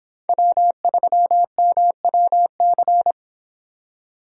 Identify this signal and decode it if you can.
Morse code, heard as a keyed tone, about 26 wpm, 700 Hz. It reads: W3MWC